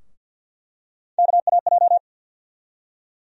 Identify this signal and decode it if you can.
Morse code, heard as a keyed tone, about 47 wpm, 700 Hz. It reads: KAJ
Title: USA Call Sign Postfixes - 3 Letters 50wpm